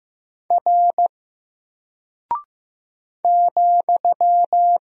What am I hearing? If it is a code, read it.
Morse code, heard as a keyed tone, about 15 wpm, 700 Hz. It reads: R ,
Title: Single Character (Letter:Number:Punctuation:Prosign) 15wpm